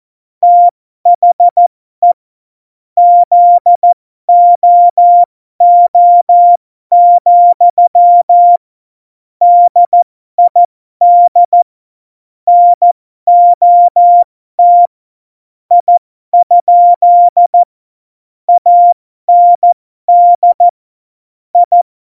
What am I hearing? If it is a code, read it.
Morse code, heard as a keyed tone, about 14 wpm, spaced as for 12 wpm, 700 Hz. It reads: THE ZOO, DID NOT I? AND I